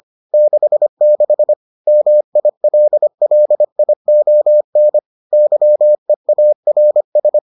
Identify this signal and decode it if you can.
Morse code, heard as a keyed tone, about 25 wpm, 600 Hz. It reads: 66 MILLION YEARS